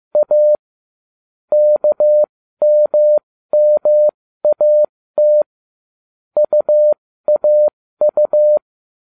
Morse code, heard as a keyed tone, about 15 wpm, spaced as for 13 wpm, 600 Hz. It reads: A KMMAT UAU